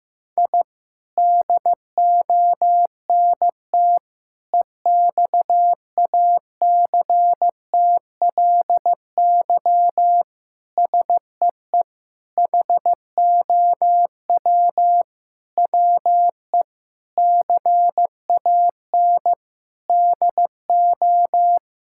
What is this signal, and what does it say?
Morse code, heard as a keyed tone, about 15 wpm, 700 Hz. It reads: I DONT EXACTLY SEE HOW WE CAN DO
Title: Sentences from Top 1000 Words - Review Part 6 - 15wpm